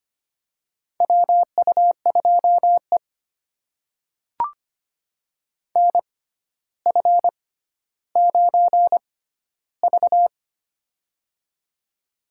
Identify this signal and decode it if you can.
Morse code, heard as a keyed tone, about 25 wpm, spaced as for 13 wpm, 700 Hz. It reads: WU2E NF9V
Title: USA Call Signs - 4 Characters - 6x Character Spacing - 25wpm